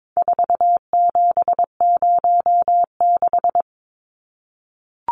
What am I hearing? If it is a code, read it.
Morse code, heard as a keyed tone, about 22 wpm, 700 Hz. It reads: 4706
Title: Numbers - 4-Digit 22wpm